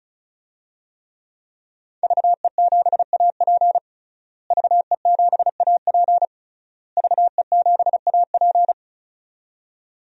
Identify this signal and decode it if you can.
Morse code, heard as a keyed tone, about 36 wpm, 700 Hz. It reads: VE7AP VE7AP VE7AP
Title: DXCC-100 Call Signs - 5 Characters - Repeated 3x - 35wpm